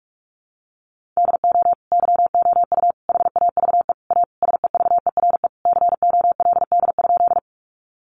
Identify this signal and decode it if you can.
Morse code, heard as a keyed tone, about 44 wpm, 700 Hz. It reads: DO YOU HAVE A SEVERE COLD?
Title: Sentences from Top 1400 Words 45wpm